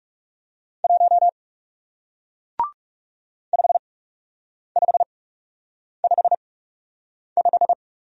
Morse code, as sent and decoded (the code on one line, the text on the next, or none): .---- / ..... ..... ..... .....
1 5555